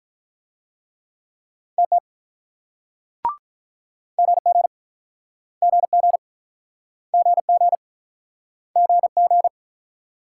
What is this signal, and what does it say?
Morse code, heard as a keyed tone, about 46 wpm, 700 Hz. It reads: TT GG GG GG GG